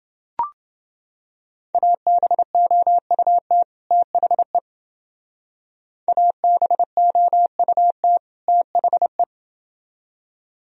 Morse code, heard as a keyed tone, about 29 wpm, 700 Hz. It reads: ABOUT THE ABOUT THE